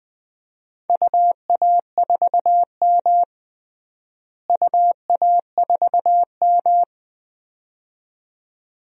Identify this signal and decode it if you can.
Morse code, heard as a keyed tone, about 20 wpm, 700 Hz. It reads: UA4M UA4M
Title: DXCC-100 Call Signs - 4 Characters - Repeated 2x - 20wpm